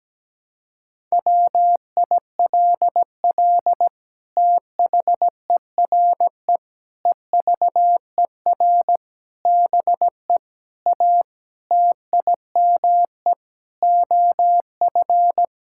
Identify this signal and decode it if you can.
Morse code, heard as a keyed tone, about 17 wpm, 700 Hz. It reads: WILL THERE EVER BE A TIME OF
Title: Sentences from Top 1000 Words - Review Part 5 - 17wpm